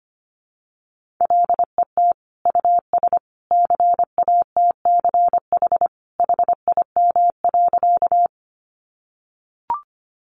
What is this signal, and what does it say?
Morse code, heard as a keyed tone, about 25 wpm, 700 Hz. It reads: LET US CATCH HIM.
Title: Sentences from Top 1500 Words 25wpm